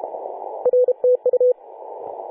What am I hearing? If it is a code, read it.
Morse code, heard as a keyed tone, about 32 wpm, 500 Hz. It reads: R TU
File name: 2013/Sep/21 2035z 3D2GC 18.072MHz CW